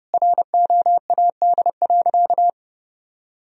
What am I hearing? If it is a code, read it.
Morse code, heard as a keyed tone, about 30 wpm, 700 Hz. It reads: ROAD.
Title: Sentences from Top 1000 Words - Review Part 1 - 30wpm